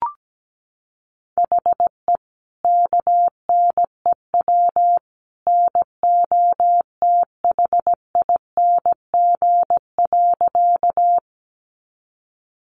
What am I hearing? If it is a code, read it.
Morse code, heard as a keyed tone, about 17 wpm, 700 Hz. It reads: HE KNEW NOTHING.
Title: Sentences from Top 500 Words 17wpm